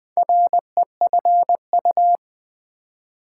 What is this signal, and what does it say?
Morse code, heard as a keyed tone, about 20 wpm, 700 Hz. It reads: REFU
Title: Build A Word - 2 Letters At A Time - 6-Letter Words - 20wpm